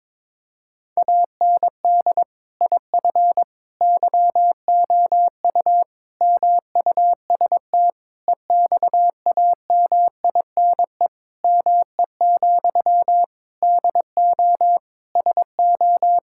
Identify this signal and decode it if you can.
Morse code, heard as a keyed tone, about 21 wpm, 700 Hz. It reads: AND IF YOU MUST EXAMINE ME, DO SO
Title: Sentences from Top 1600 Words 22wpm